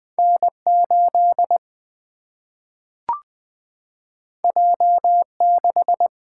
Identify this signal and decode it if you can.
Morse code, heard as a keyed tone, about 20 wpm, 700 Hz. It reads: N 8 J 6
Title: Single-Letter Single-Number 20wpm